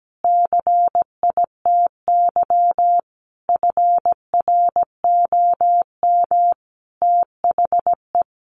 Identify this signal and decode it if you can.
Morse code, heard as a keyed tone, about 17 wpm, 700 Hz. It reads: CITY FROM THE
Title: Sentences from Top 1200 Words 17wpm